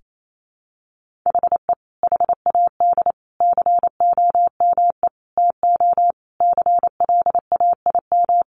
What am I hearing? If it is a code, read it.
Morse code, heard as a keyed tone, about 28 wpm, 700 Hz. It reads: HE HAD COME TO CLAIM
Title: Sentences from Top 1500 Words 28wpm